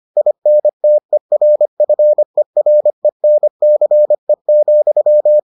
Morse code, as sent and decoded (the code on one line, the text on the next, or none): .. -. - . .-. ..-. .-. -. -.-. --..--
INTERFERENCE,